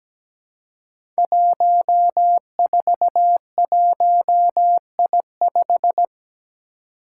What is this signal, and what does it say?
Morse code, heard as a keyed tone, about 17 wpm, 700 Hz. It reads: 141I5
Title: Five Challenging Letter-Number Combinations 17wpm